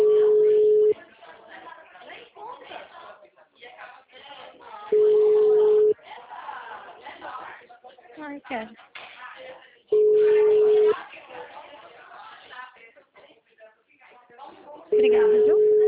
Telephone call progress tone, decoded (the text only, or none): ringback tone